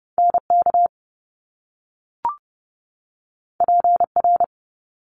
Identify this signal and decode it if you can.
Morse code, heard as a keyed tone, about 30 wpm, 700 Hz. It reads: NK PR
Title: Top Two Letter Combinations in Words 30wpm